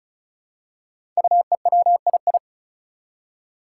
Morse code, heard as a keyed tone, about 36 wpm, 700 Hz. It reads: UEWII
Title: Five Challenging Letter-Number Combinations 35wpm